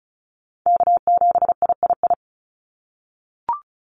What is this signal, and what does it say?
Morse code, heard as a keyed tone, about 36 wpm, 700 Hz. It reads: K7III
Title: NA SOTA Chaser Callsigns 35wpm